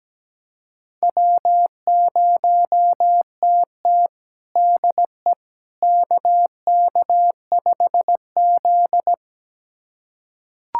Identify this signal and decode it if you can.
Morse code, heard as a keyed tone, about 17 wpm, 700 Hz. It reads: W0TT DE KK5Z
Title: USA Callsign DE Callsign - 4 Characters - 17wpm